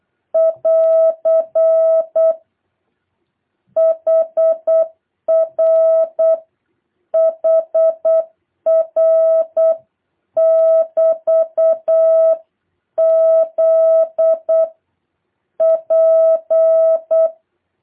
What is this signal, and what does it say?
Morse code, heard as a keyed tone, about 8 wpm, 650 Hz. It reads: <AR> HR HR <BT> Z P
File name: jh8cbh / 2023cw